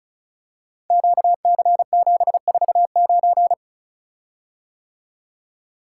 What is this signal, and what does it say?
Morse code, heard as a keyed tone, about 36 wpm, 700 Hz. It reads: QC749
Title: Code Groups - 5 Letter:Numbers - 35wpm